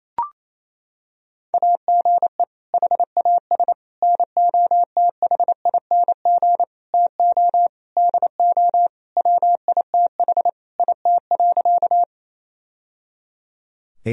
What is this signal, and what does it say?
Morse code, heard as a keyed tone, about 27 wpm, 700 Hz. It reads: AGE HAS NOTHING TO DO WITH IT.
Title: Sentences from Top 1000 Words - Review Part 3 - 28wpm